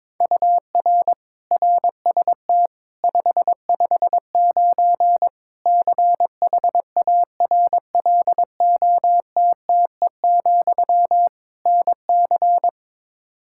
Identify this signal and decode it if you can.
Morse code, heard as a keyed tone, about 22 wpm, 700 Hz. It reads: UR RST 559 CHARLOTTE, NC
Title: Practice QSO - Exchange 1 - Beginning - 22wpm